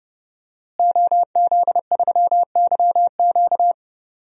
Morse code, heard as a keyed tone, about 30 wpm, 700 Hz. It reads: OZ3YQ
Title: DXCC-100 Call Signs - 5 Characters - Repeated 2x - 30wpm